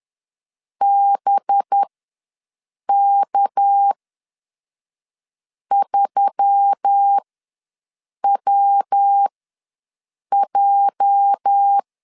Morse code, heard as a keyed tone, about 10 wpm, spaced as for 5 wpm, 800 Hz. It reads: BK3WJ
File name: LEZIONI / 10 SIGNS / 10A SLASH ws 10wpm 4121